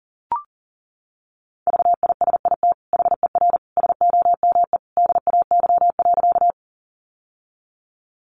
Morse code, heard as a keyed tone, about 37 wpm, 700 Hz. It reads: VISIT HER SOME DAY.
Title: Sentences from Top 900 Words 40wpm